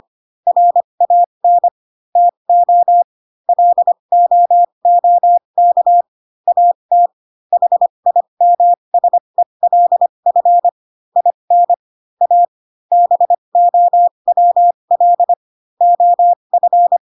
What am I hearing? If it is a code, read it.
Morse code, heard as a keyed tone, about 25 wpm, spaced as for 22 wpm, 700 Hz. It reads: RAN TO LOOK AT HIMSELF IN A BOWL OF